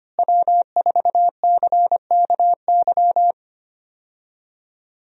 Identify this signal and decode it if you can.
Morse code, heard as a keyed tone, about 25 wpm, 700 Hz. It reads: W4CKY
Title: USA Call Signs - 5 Characters - Repeated 3x - 25wpm